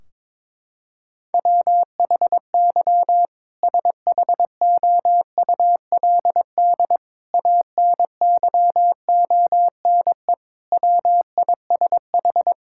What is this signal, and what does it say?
Morse code, heard as a keyed tone, about 22 wpm, 700 Hz. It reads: WHY SHOULD ANYONE WISH
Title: Sentences from Top 1600 Words 22wpm